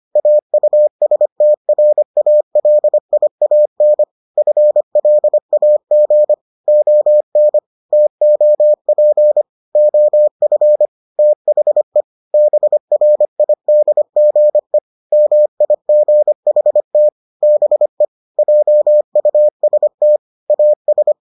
Morse code, heard as a keyed tone, about 25 wpm, 600 Hz. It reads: AUSTRALIAN FLAG ON TOP OF THE BRIDGE MIGHT BE JUST AS